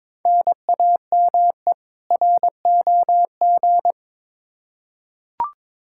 Morse code, heard as a keyed tone, about 22 wpm, 700 Hz. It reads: NAME ROG
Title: Practice QSO - Exchange 1 - Beginning - 22wpm